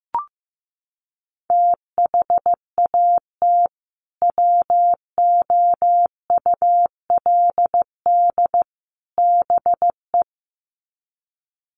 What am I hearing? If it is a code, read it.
Morse code, heard as a keyed tone, about 15 wpm, 700 Hz. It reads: THAT WOULD BE
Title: Sets of 3 Words 15wpm